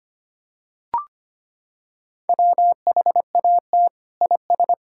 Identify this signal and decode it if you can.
Morse code, heard as a keyed tone, about 25 wpm, 700 Hz. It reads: WHAT IS
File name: practice / 25wpm / Sentences with 2-Word Phrases - 25wpm